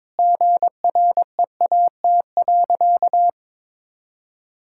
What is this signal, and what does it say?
Morse code, heard as a keyed tone, about 22 wpm, 700 Hz. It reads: GREAT.
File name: practice / 22wpm / Sentences with 2-Word Phrases - 22wpm